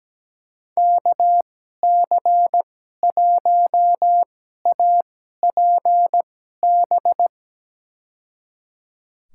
Morse code, heard as a keyed tone, about 17 wpm, spaced as for 12 wpm, 700 Hz. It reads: KC1APB
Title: USA Call Signs - 6 Characters - 2x Character Spacing - 17wpm